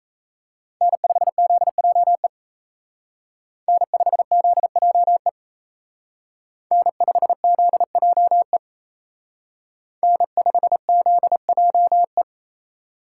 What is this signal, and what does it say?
Morse code, heard as a keyed tone, about 34 wpm, 700 Hz. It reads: N5ZJE N5ZJE N5ZJE N5ZJE
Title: Speed Racing - USA 1x3 Call Signs 28wpm